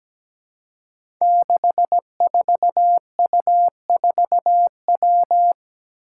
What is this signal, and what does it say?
Morse code, heard as a keyed tone, about 17 wpm, 700 Hz. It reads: 64U4W
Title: Five Challenging Letter-Number Combinations 17wpm